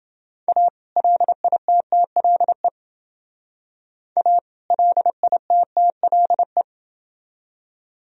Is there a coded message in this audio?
Yes: Morse code from 0.5 to 6.6 s, heard as a keyed tone, about 32 wpm, 700 Hz. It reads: A LITTLE A LITTLE